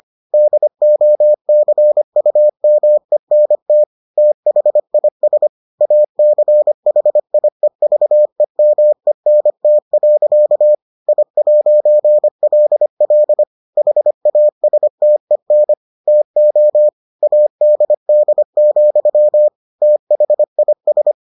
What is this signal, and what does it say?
Morse code, heard as a keyed tone, about 25 wpm, 600 Hz. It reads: DOCUMENT THIS ACHIEVEMENT. I'LL HASTEN TO ADD, THIS